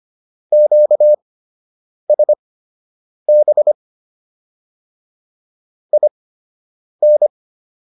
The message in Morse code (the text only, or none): QSB IN